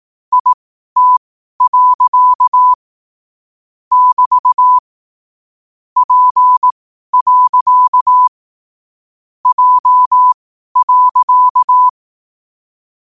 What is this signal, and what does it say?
Morse code, heard as a keyed tone, about 18 wpm, spaced as for 12 wpm, 1000 Hz. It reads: IT. <BT> P. J.
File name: qotdfiles / 12wpm